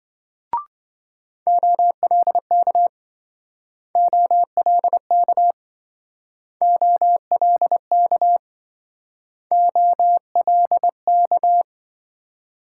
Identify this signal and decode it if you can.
Morse code, heard as a keyed tone, about 26 wpm, 700 Hz. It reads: OLK OLK OLK OLK